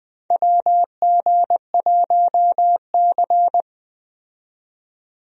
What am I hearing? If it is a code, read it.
Morse code, heard as a keyed tone, about 20 wpm, 700 Hz. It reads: WG1C
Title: USA Call Signs - 4 Characters - 20wpm